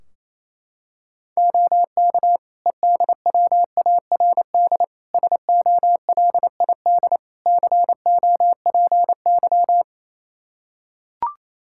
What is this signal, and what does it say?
Morse code, heard as a keyed tone, about 28 wpm, 700 Hz. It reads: OK EDWARD SOLID COPY
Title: Practice QSO - Exchange 3 - More Info - 28wpm